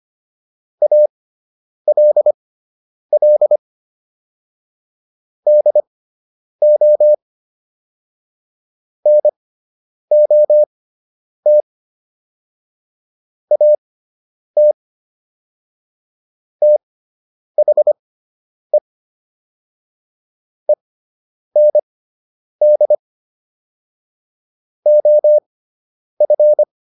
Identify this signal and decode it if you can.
Morse code, heard as a keyed tone, about 25 wpm, spaced as for 9 wpm, 600 Hz. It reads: ALL DO NOT AT THE END OF